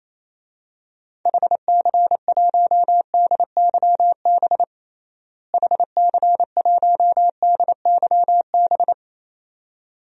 Morse code, heard as a keyed tone, about 28 wpm, 700 Hz. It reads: HC1DYB HC1DYB
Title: DXCC-100 Call Signs - 6 Characters - Repeated 2x - 28wpm